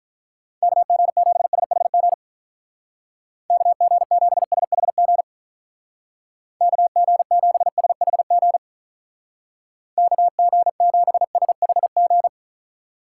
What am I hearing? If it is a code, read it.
Morse code, heard as a keyed tone, about 44 wpm, 700 Hz. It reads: KG7SHG KG7SHG KG7SHG KG7SHG